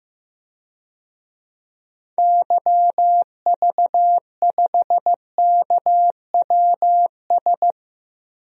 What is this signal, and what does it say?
Morse code, heard as a keyed tone, about 15 wpm, 700 Hz. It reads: YV5KWS
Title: DXCC-100 Call Signs - 6 Characters - 15wpm